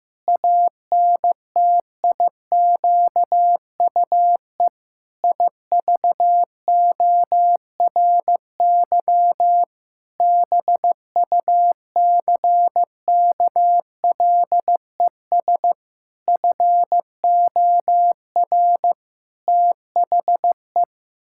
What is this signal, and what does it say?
Morse code, heard as a keyed tone, about 15 wpm, 700 Hz. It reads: ANTIQUE IVORY BUCKLES FOR THE
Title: Pangram Challenge v2 - 15wpm